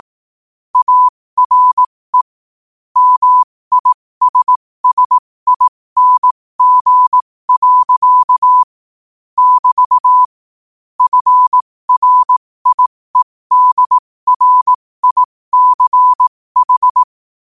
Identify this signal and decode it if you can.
Morse code, heard as a keyed tone, about 18 wpm, spaced as for 15 wpm, 1000 Hz. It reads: ARE MISSING. <BT> FRIEDRICH
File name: qotdfiles / 15wpm